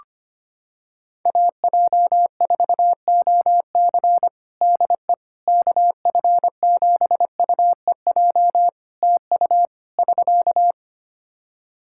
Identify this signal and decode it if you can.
Morse code, heard as a keyed tone, about 25 wpm, 700 Hz. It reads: AJ4OC DE KF7UEJ TU <SK>
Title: Practice QSO - Exchange 4 - Ending - 25wpm